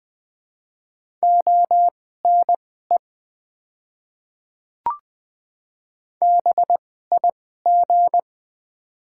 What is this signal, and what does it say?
Morse code, heard as a keyed tone, about 21 wpm, spaced as for 15 wpm, 700 Hz. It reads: ONE BIG
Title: 2x Character Spacing - Top 200 Words - 20wpm